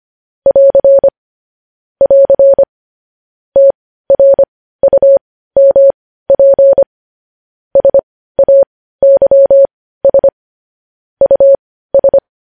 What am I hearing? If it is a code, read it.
Morse code, heard as a keyed tone, about 25 wpm, spaced as for 15 wpm, 550 Hz. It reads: <AR> <AR> TRUMP SAYS US
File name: podcast / News headlines-1x-25x15